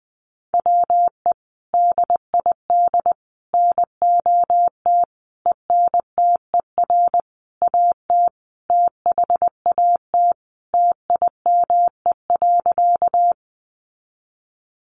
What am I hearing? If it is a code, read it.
Morse code, heard as a keyed tone, about 20 wpm, 700 Hz. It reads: WE DID NOT ENTER AT THAT TIME.
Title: Sentences from Top 1300 Words 20wpm